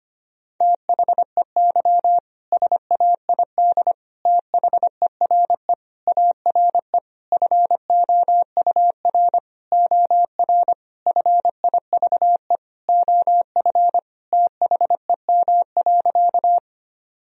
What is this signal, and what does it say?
Morse code, heard as a keyed tone, about 25 wpm, 700 Hz. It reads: THEY SAID THERE ARE FOUR OR FIVE OF THEM.
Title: Sentences from Top 1000 Words - Review Part 2 - 25wpm